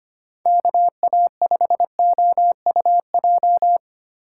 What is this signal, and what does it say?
Morse code, heard as a keyed tone, about 25 wpm, 700 Hz. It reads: KA5OUJ
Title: USA Call Signs - 6 Characters - 8x Character Spacing - 25wpm